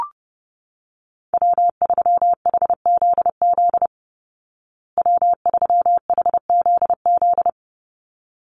Morse code, heard as a keyed tone, about 30 wpm, 700 Hz. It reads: W3HZZ W3HZZ
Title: Call Signs Repeated 3x - 30wpm